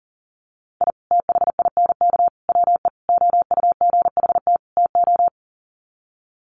Morse code, heard as a keyed tone, about 40 wpm, 700 Hz. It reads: I THINK WE OUGHT TO
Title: Sets of 5 Words Encore 40wpm